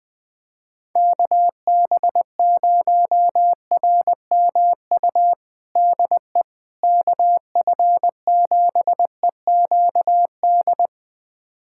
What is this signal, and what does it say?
Morse code, heard as a keyed tone, about 21 wpm, 700 Hz. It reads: KB0RMU DE KF7EQD
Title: USA Callsign DE Callsign - 6 Characters - 20wpm